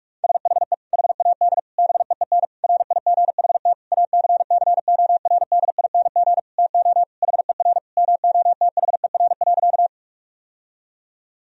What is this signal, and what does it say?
Morse code, heard as a keyed tone, about 48 wpm, 700 Hz. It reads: SHE HAD BEEN RIGHT ACCORDING TO HER MOTHER.
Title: Sentences from Top 1000 Words - Review Part 4 - 45wpm